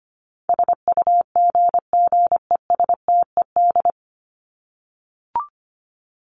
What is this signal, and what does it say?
Morse code, heard as a keyed tone, about 25 wpm, 700 Hz. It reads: SUGGESTED
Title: ED Suffix - Top 100 Words with Base Words 25wpm